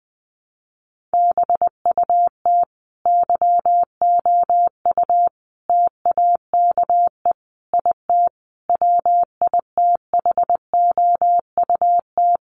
Morse code, heard as a keyed tone, about 20 wpm, 700 Hz. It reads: BUT YOU TAKE IT WITHOUT
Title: Sentences from Top 1200 Words 20wpm